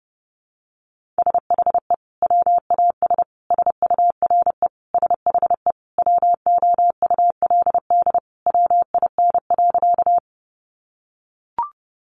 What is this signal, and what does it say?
Morse code, heard as a keyed tone, about 30 wpm, 700 Hz. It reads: SHE WAS SURE SHE WOULD WIN.
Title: Sentences from Top 1500 Words 30wpm